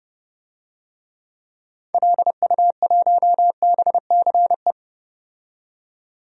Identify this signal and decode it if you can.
Morse code, heard as a keyed tone, about 30 wpm, 700 Hz. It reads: LU1BCE
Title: DXCC-100 Call Signs - 6 Characters - 30wpm